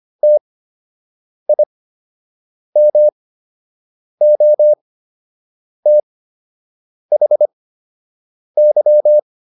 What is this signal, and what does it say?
Morse code, heard as a keyed tone, about 25 wpm, spaced as for 7 wpm, 600 Hz. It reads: TIMOTHY